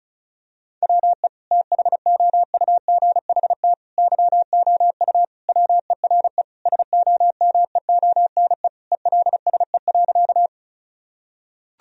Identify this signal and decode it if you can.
Morse code, heard as a keyed tone, about 36 wpm, 700 Hz. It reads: WE THOUGHT YOU WERE SOMEONE ELSE.
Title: Sentences from Top 1000 Words - Review Part 4 - 35wpm